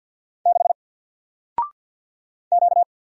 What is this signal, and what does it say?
Morse code, heard as a keyed tone, about 50 wpm, 700 Hz. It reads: 6 Q
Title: Single Letter-Number 50wpm